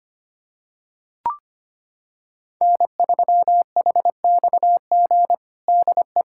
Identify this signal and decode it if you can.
Morse code, heard as a keyed tone, about 25 wpm, 700 Hz. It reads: N3HXG DE
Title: Practice QSO - Exchange 4 - Ending - 25wpm